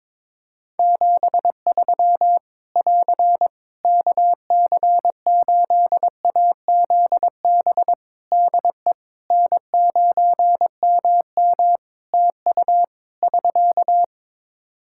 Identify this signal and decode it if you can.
Morse code, heard as a keyed tone, about 22 wpm, 700 Hz. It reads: 73 <AR> KC8AZB DE N9MM TU <SK>
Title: K1USN SST QSO Practice 22wpm